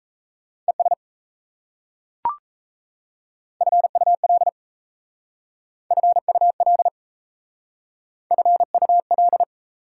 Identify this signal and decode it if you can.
Morse code, heard as a keyed tone, about 37 wpm, 700 Hz. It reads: ES FUL FUL FUL